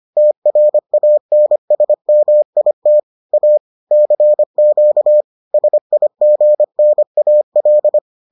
Morse code, heard as a keyed tone, about 25 wpm, 600 Hz. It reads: TRANSMIT A CQ SIGNAL